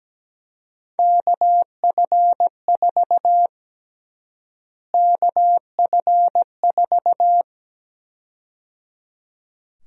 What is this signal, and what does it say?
Morse code, heard as a keyed tone, about 17 wpm, 700 Hz. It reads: KF4 KF4
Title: USA Call Sign Prefixes - Repeated 2x - 17wpm